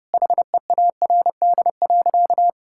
Morse code, heard as a keyed tone, about 32 wpm, 700 Hz. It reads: HEARD.
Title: Sentences from Top 1600 Words 30wpm